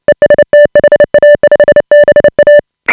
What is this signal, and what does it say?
Morse code, heard as a keyed tone, about 29 wpm, 590 Hz. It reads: EST HA5BA
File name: AUDIO2018 / HA5BA CW